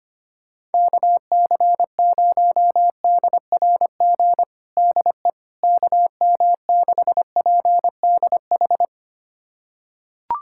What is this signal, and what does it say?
Morse code, heard as a keyed tone, about 25 wpm, 700 Hz. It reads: KC0DRG DE KM6PDH
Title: USA Callsign DE Callsign - 6 Characters - 25wpm